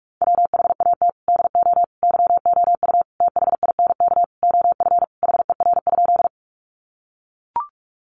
Morse code, heard as a keyed tone, about 44 wpm, 700 Hz. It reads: WHAT DO YOU THINK OF HER?
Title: Sentences from Top 200 Words 45wpm